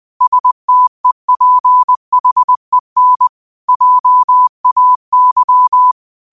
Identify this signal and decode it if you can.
Morse code, heard as a keyed tone, about 20 wpm, 1000 Hz. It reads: STEPHEN JAY